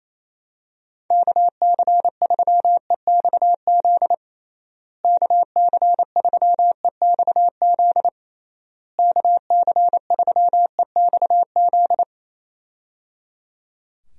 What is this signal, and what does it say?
Morse code, heard as a keyed tone, about 28 wpm, 700 Hz. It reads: KC3EXZ KC3EXZ KC3EXZ
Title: USA Call Signs - 6 Characters - Repeated 3x - 28wpm